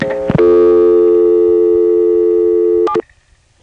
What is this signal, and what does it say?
Dial tone (North America), 350 + 440 Hz.
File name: phonetrips / CoinControlHQ